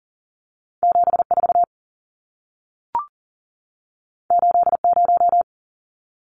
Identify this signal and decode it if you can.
Morse code, heard as a keyed tone, about 40 wpm, 700 Hz. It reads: 74 80
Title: Numbers - 2-Digit 40wpm